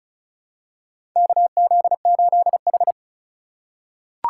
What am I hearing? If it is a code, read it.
Morse code, heard as a keyed tone, about 36 wpm, 700 Hz. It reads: KZ8H
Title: USA Call Signs - 4 Characters - 3x Character Spacing - 35wpm